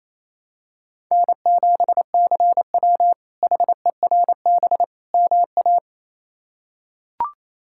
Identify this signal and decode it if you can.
Morse code, heard as a keyed tone, about 27 wpm, 700 Hz. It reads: N7CW HERB MA
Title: K1USN SST QSO Practice - 28wpm